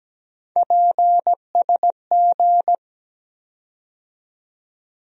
Morse code, heard as a keyed tone, about 17 wpm, 700 Hz. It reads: PSG
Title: USA Call Sign Postfixes - 3 Letters 17wpm